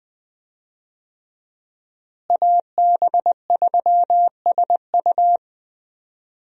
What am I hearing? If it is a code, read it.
Morse code, heard as a keyed tone, about 21 wpm, 700 Hz. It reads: AB3SU
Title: USA Call Signs - 5 Characters - Repeated 3x - 20wpm